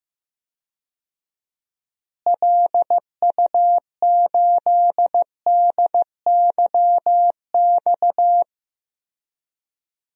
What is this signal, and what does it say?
Morse code, heard as a keyed tone, about 16 wpm, 700 Hz. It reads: LU8DYX
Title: DXCC-100 Call Signs - 6 Characters - Repeated 3x - 15wpm